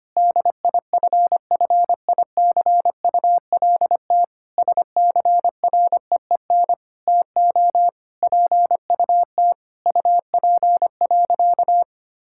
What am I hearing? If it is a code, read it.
Morse code, heard as a keyed tone, about 25 wpm, 700 Hz. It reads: DIFFICULT SCREEN TO PUT UP.
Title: Sentences from Top 1000 Words - Review Part 6 - 25wpm